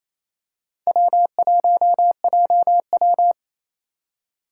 Morse code, heard as a keyed tone, about 27 wpm, 700 Hz. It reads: W1JW
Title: USA Call Signs - 4 Characters - 4x Character Spacing - 28wpm